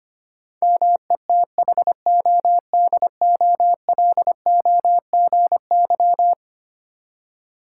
Morse code, heard as a keyed tone, about 25 wpm, 700 Hz. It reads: METHODOLOGY
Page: morse-code-ninja-practice-25wpm